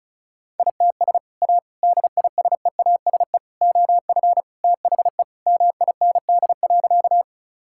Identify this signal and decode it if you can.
Morse code, heard as a keyed tone, about 36 wpm, 700 Hz. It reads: ITS A DISEASE OF THE MIND.
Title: Sentences from Top 1000 Words - Review Part 6 - 35wpm